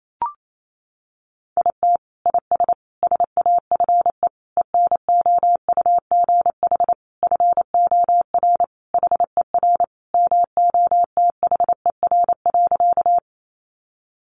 Morse code, heard as a keyed tone, about 28 wpm, 700 Hz. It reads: IT IS SAFE ENOUGH FOR HER MOTHER.
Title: Sentences from Top 900 Words 28wpm